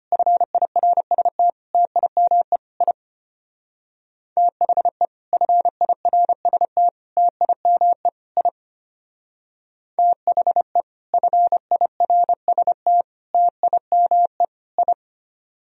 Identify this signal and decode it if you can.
Morse code, heard as a keyed tone, about 33 wpm, 700 Hz. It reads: FIRST TIME I THE FIRST TIME I THE FIRST TIME I